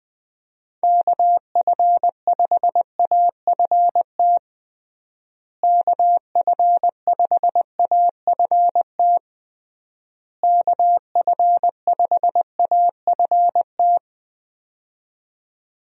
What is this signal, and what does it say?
Morse code, heard as a keyed tone, about 20 wpm, 700 Hz. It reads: KF5AFT KF5AFT KF5AFT
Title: USA Call Signs - 6 Characters - Repeated 3x - 20wpm